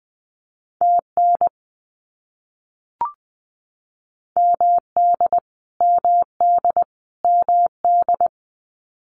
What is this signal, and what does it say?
Morse code, heard as a keyed tone, about 20 wpm, 700 Hz. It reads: TN MD MD MD
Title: US States Repeated 3x 20wpm